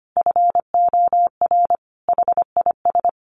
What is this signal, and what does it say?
Morse code, heard as a keyed tone, about 25 wpm, 700 Hz. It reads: FOR HIS
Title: Sentences from Top 1300 Words 25wpm